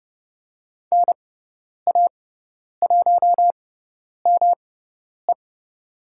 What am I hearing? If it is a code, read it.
Morse code, heard as a keyed tone, about 30 wpm, spaced as for 10 wpm, 700 Hz. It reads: NA1ME
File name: Lesson 18g - Letter m - Call Signs 30wpm